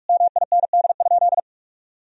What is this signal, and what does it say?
Morse code, heard as a keyed tone, about 41 wpm, 700 Hz. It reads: MIND?
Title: Sentences from Top 1000 Words - Review Part 3 - 45wpm